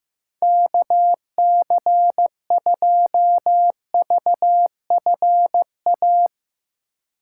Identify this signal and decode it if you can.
Morse code, heard as a keyed tone, about 15 wpm, 700 Hz. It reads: KC2VFA
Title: USA Call Signs - 6 Characters - 6x Character Spacing - 15wpm